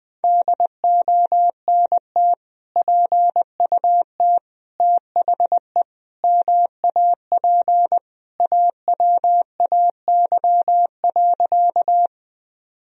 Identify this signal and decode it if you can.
Morse code, heard as a keyed tone, about 20 wpm, 700 Hz. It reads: DONT PUT THE MAP AWAY.
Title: Sentences from Top 1000 Words - Review Part 2 - 20wpm